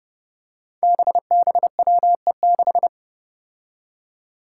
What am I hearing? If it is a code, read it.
Morse code, heard as a keyed tone, about 30 wpm, 700 Hz. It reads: BBWE6
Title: Five Challenging Letter-Number Combinations 30wpm